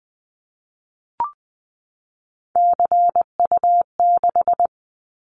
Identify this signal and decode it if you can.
Morse code, heard as a keyed tone, about 20 wpm, 700 Hz. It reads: CU6
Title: Callsign Prefixes for Top 100 DXCC - 20wpm